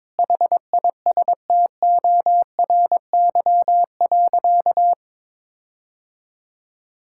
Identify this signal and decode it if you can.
Morse code, heard as a keyed tone, about 22 wpm, 700 Hz. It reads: HISTORY.
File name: Sentences with 7-Word Phrases - 22wpm